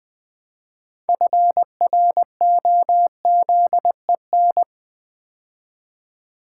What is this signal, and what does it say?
Morse code, heard as a keyed tone, about 20 wpm, 700 Hz. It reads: FROZEN
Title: Build A Word - 3 Letters At A Time - 6-Letter Words - 20wpm